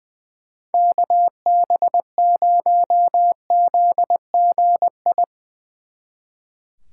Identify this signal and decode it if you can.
Morse code, heard as a keyed tone, about 24 wpm, 700 Hz. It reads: KB0ZGI
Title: Speed Racing - USA 2x3 Call Signs 20wpm